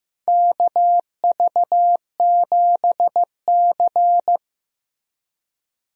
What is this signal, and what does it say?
Morse code, heard as a keyed tone, about 15 wpm, 700 Hz. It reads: KV7C
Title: USA Call Signs - 4 Characters - Repeated 3x - 15wpm